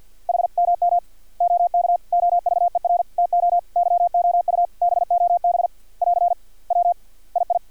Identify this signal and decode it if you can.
Morse code, heard as a keyed tone, about 50 wpm, 700 Hz. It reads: HMM OK OVER TO YOU BOB <AR> W II
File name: archived / CRSnet